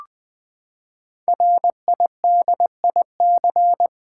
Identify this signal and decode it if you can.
Morse code, heard as a keyed tone, about 20 wpm, 700 Hz. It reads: RIDIC
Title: Build A Word - 5 Letters At A Time - 10-Letter Words - 20wpm